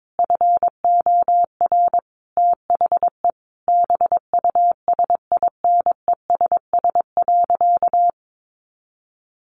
Morse code, heard as a keyed tone, about 22 wpm, 700 Hz. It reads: FOR THE BUSINESS.
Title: Sentences from Top 1400 Words 22wpm